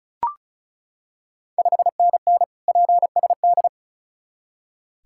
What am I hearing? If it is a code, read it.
Morse code, heard as a keyed tone, about 36 wpm, 690 Hz. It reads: 5NN PSD
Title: Ontario QSO Party Exchange 35wpm